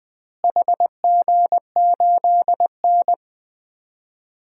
Morse code, heard as a keyed tone, about 20 wpm, 700 Hz. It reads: HG8N
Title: DXCC-100 Call Signs - 4 Characters - Repeated 2x - 20wpm